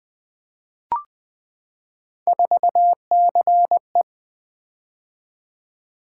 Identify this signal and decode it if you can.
Morse code, heard as a keyed tone, about 20 wpm, 700 Hz. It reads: 4CE
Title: Three Letter-Number Combinations - 20wpm